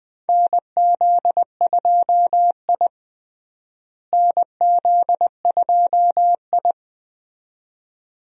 Morse code, heard as a keyed tone, about 21 wpm, 700 Hz. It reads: NZ2I NZ2I
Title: USA Call Signs - 4 Characters - Repeated 2x - 20wpm